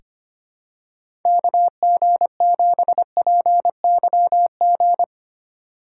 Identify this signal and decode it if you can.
Morse code, heard as a keyed tone, about 25 wpm, 700 Hz. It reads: KG7PYG
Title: USA Call Signs - 2x Character Spacing - 25wpm